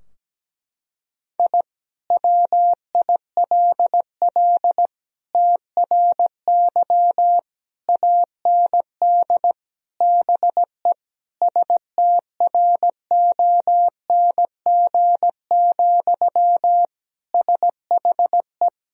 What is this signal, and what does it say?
Morse code, heard as a keyed tone, about 17 wpm, 700 Hz. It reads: I WILL TRY AND BE STRONG, SHE
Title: Sentences from Top 1000 Words - Review Part 3 - 17wpm